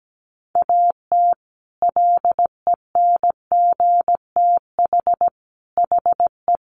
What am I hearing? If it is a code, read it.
Morse code, heard as a keyed tone, about 17 wpm, 700 Hz. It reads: AT LENGTH HE
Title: Sentences from Top 1000 Words 17wpm